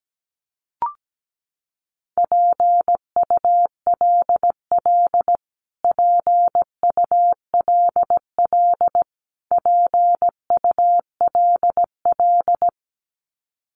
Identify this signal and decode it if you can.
Morse code, heard as a keyed tone, about 17 wpm, 700 Hz. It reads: PULL PULL PULL
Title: Top 500 Words Repeated 17wpm